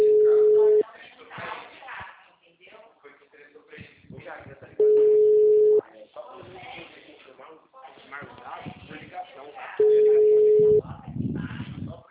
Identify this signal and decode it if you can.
Ringback tone (Europe, 425 Hz), with speech.